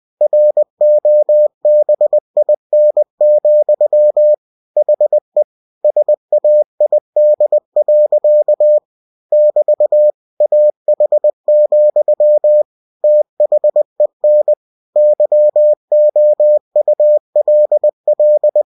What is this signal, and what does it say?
Morse code, heard as a keyed tone, about 20 wpm, 600 Hz. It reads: ROBIN, HE SAID. <BT> AH, THEN YOULL
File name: cw-books / winnie-the-pooh / winnie-20wpm-page-0024